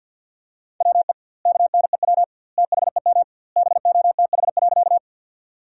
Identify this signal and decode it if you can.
Morse code, heard as a keyed tone, about 50 wpm, 700 Hz. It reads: WE KNEW THEM BOTH.
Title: 2x Word-Spacing - Sentences from Top 400 Words - 50wpm